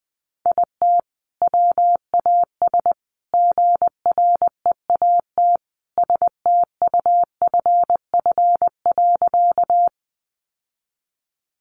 Morse code, heard as a keyed tone, about 20 wpm, 700 Hz. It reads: IT WAS GREAT STUFF.
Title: Sentences from Top 1400 Words 20wpm